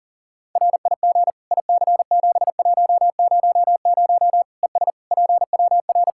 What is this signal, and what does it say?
Morse code, heard as a keyed tone, about 40 wpm, 700 Hz. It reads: RIG IC7100 ES PWR
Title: K1USN SST QSO Practice 40wpm